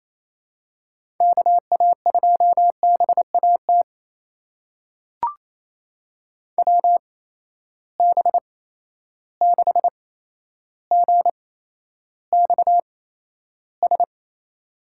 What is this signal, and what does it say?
Morse code, heard as a keyed tone, about 28 wpm, spaced as for 12 wpm, 700 Hz. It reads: KA2BAT WB6GXS
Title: USA Call Signs - 6 Characters - 8x Character Spacing - 28wpm